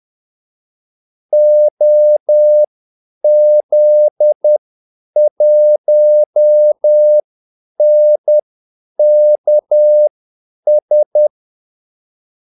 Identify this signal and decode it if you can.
Morse code, heard as a keyed tone, about 10 wpm, spaced as for 8 wpm, 600 Hz. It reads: OZ1NKS
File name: Callsigns-10-8wpm